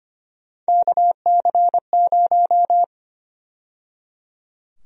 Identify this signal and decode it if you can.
Morse code, heard as a keyed tone, about 25 wpm, 700 Hz. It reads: KC0
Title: USA Call Sign Prefixes - 25wpm